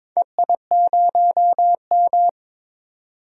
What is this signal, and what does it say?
Morse code, heard as a keyed tone, about 22 wpm, 700 Hz. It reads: EI0M
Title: DXCC-100 Call Signs - 4 Characters - 22wpm